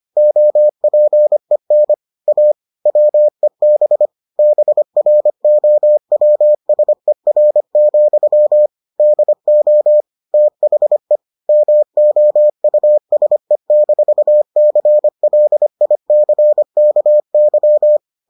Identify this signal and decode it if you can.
Morse code, heard as a keyed tone, about 25 wpm, 600 Hz. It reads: OPEN A WEB BROWSER, DO THE MOUSE-CLICKY